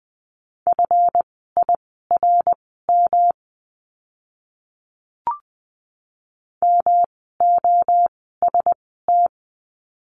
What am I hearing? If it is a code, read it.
Morse code, heard as a keyed tone, about 20 wpm, spaced as for 14 wpm, 700 Hz. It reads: FIRM MOST
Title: 139 Lesson 23f - Letter f - Words 20wpm